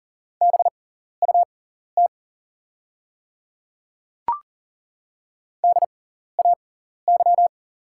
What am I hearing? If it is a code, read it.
Morse code, heard as a keyed tone, about 40 wpm, spaced as for 14 wpm, 690 Hz. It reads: BUT DAY
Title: 6x Character Spacing - Top 100 Words - 40wpm